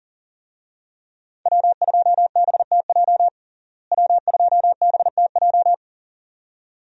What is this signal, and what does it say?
Morse code, heard as a keyed tone, about 40 wpm, 700 Hz. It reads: W2BTJ W2BTJ
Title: USA Call Signs - 5 Characters - Repeated 2x - 40wpm